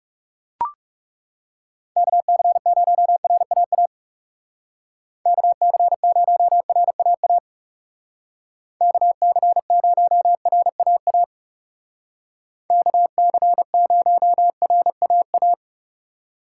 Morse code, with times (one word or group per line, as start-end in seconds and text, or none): 2.0-3.9 KC0RAA
5.3-7.4 KC0RAA
8.8-11.2 KC0RAA
12.7-15.5 KC0RAA